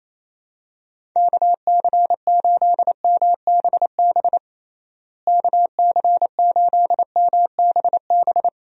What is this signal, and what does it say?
Morse code, heard as a keyed tone, about 28 wpm, 700 Hz. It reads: KC8MBB KC8MBB
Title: USA Call Signs - 6 Characters - Repeated 2x - 28wpm